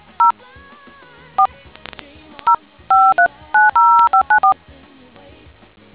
DTMF, heard as telephone touch tones, tones of about 110 ms. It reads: 0 4 0 5390594